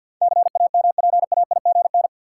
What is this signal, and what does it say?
Morse code, heard as a keyed tone, about 47 wpm, 700 Hz. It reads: CAMPAIGN
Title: Sentences from Top 1200 Words - New Word Drill - 50wpm